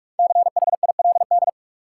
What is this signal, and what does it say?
Morse code, heard as a keyed tone, about 46 wpm, 700 Hz. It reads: CHILD
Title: Five Letter Words - 45wpm